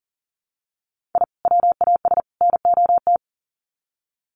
Morse code, heard as a keyed tone, about 40 wpm, 700 Hz. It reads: I WAS NOT
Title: Sets of 3 Words 40wpm